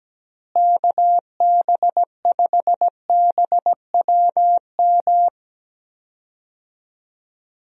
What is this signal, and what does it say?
Morse code, heard as a keyed tone, about 17 wpm, 700 Hz. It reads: KB5BWM
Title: USA Call Signs - 6 Characters - Repeated 2x - 17wpm